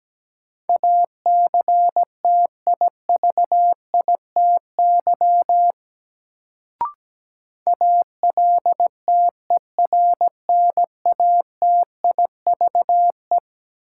Morse code, heard as a keyed tone, about 17 wpm, 700 Hz. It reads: ACTIVITY ALTERNATIVE
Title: Sentences from Top 1000 Words - Review Part 4 - New Word Drill - 17wpm